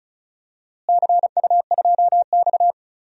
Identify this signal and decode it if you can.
Morse code, heard as a keyed tone, about 34 wpm, 700 Hz. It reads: CU2X
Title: DXCC-100 Call Signs - 4 Characters - Repeated 3x - 35wpm